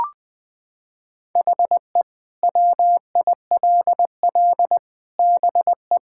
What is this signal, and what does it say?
Morse code, heard as a keyed tone, about 20 wpm, 700 Hz. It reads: HE WILL BE
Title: Sentences from Top 1600 Words 20wpm